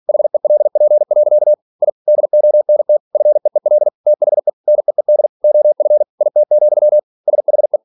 Morse code, heard as a keyed tone, about 46 wpm, 600 Hz. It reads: HELP. I DONT FEEL THE NEED OF IT, SHE